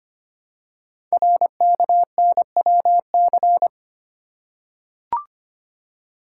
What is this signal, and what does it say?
Morse code, heard as a keyed tone, about 25 wpm, 700 Hz. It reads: RKNWC
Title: Code Groups - 5 Letters - 25wpm